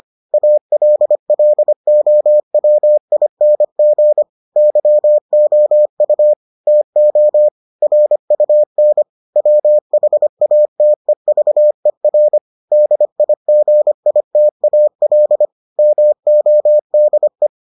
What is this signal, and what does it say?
Morse code, heard as a keyed tone, about 25 wpm, 600 Hz. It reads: ALLOWING YOU TO RUN WHATEVER DIGITAL MODE